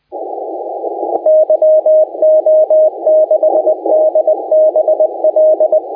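Morse code, heard as a keyed tone, about 21 wpm, 590 Hz. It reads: YO6DBL
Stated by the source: Morse contest audio recorded by LA4C during the SAC CW 2018 contest.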